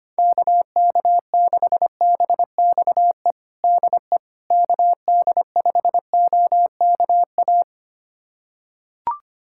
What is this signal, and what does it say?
Morse code, heard as a keyed tone, about 25 wpm, 700 Hz. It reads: KK6BXE DE KD5OKA
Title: USA Callsign DE Callsign - 6 Characters - 25wpm